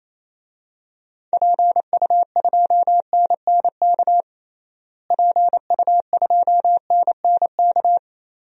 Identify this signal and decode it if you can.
Morse code, heard as a keyed tone, about 28 wpm, 700 Hz. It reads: PU2NNK PU2NNK
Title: DXCC-100 Call Signs - 6 Characters - Repeated 2x - 28wpm